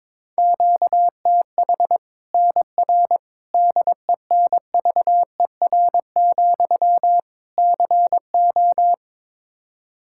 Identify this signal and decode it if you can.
Morse code, heard as a keyed tone, about 22 wpm, 700 Hz. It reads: QTH NR DENVER, CO
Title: K1USN SST QSO Practice 22wpm